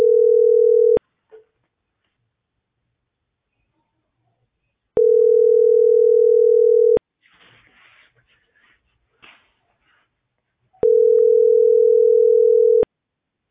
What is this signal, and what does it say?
Ringback tone (North America), 440 + 480 Hz.